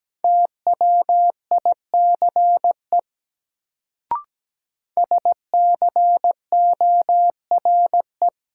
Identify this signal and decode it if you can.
Morse code, heard as a keyed tone, about 17 wpm, 700 Hz. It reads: TWICE SCORE
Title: Sentences from Top 1600 Words - New Word Drill Alt - 17wpm